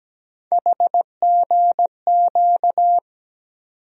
Morse code, heard as a keyed tone, about 17 wpm, 700 Hz. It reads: HGQ
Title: USA Call Sign Postfixes - 3 Letters 17wpm